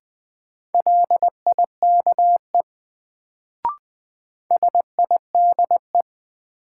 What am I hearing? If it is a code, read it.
Morse code, heard as a keyed tone, about 20 wpm, 700 Hz. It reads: LIKE SIDE
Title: Sets of 2 Words - New Word Drill - 20wpm